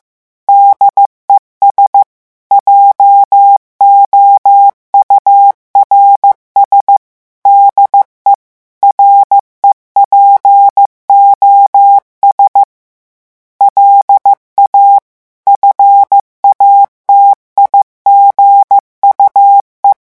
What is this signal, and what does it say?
Morse code, heard as a keyed tone, about 15 wpm, 800 Hz. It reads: DES JOURS DE REPOS LA FATIGUE
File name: morse mp3 / revision9 citations humoristiques vitesse 14 mots minute